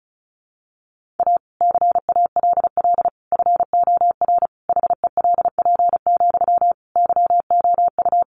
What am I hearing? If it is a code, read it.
Morse code, heard as a keyed tone, about 34 wpm, 700 Hz. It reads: A CALL FOR HELP, YOU
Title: Sentences from Top 300 Words 35wpm